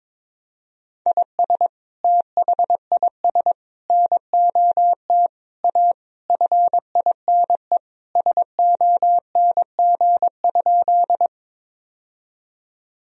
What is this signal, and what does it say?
Morse code, heard as a keyed tone, about 22 wpm, 700 Hz. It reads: IS THIS NOT A FINE SONG?
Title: Sentences from Top 1000 Words - Review Part 2 - 22wpm